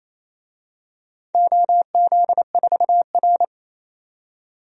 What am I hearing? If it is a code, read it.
Morse code, heard as a keyed tone, about 29 wpm, 700 Hz. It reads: OZ4R
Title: DXCC-100 Call Signs - 4 Characters - 28wpm